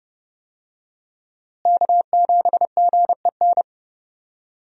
Morse code, heard as a keyed tone, about 30 wpm, 700 Hz. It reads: K7GEN
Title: USA Call Signs - 5 Characters - Repeated 3x - 30wpm